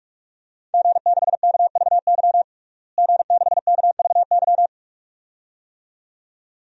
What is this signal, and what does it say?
Morse code, heard as a keyed tone, about 42 wpm, 700 Hz. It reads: G6KVY G6KVY
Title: DXCC-100 Call Signs - 5 Characters - Repeated 2x - 45wpm